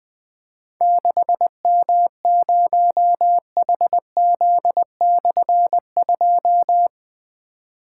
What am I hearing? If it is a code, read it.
Morse code, heard as a keyed tone, about 21 wpm, 700 Hz. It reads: 6M0HZ/2
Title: Difficult Call Signs 20wpm